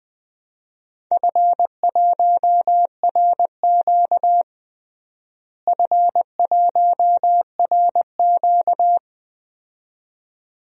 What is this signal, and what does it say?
Morse code, heard as a keyed tone, about 20 wpm, 700 Hz. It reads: F1RQ F1RQ
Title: DXCC-100 Call Signs - 4 Characters - Repeated 2x - 20wpm